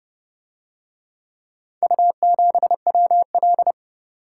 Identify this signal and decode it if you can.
Morse code, heard as a keyed tone, about 26 wpm, 700 Hz. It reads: U7WL